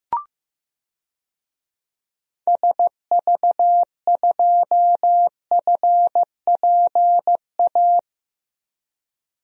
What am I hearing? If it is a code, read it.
Morse code, heard as a keyed tone, about 15 wpm, 700 Hz. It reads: SV2FPA
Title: DXCC-100 Call Signs - 6 Characters - Repeated 3x - 15wpm